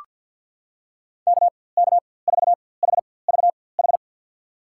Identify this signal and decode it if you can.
Morse code, heard as a keyed tone, about 47 wpm, spaced as for 23 wpm, 700 Hz. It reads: KK4HVH
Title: USA Call Signs - 4x Character Spacing - 50wpm